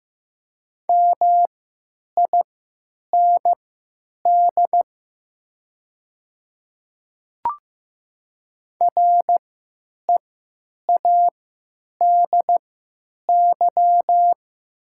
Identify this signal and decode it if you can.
Morse code, heard as a keyed tone, about 15 wpm, spaced as for 9 wpm, 700 Hz. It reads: MIND READY
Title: 3x Character Spacing - Top 500 Words - 15wpm